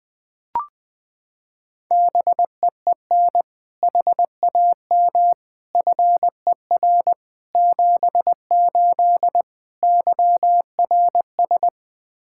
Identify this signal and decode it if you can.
Morse code, heard as a keyed tone, about 19 wpm, 700 Hz. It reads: BEEN HAM FER 78 YRS
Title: K1USN SST QSO Practice 20wpm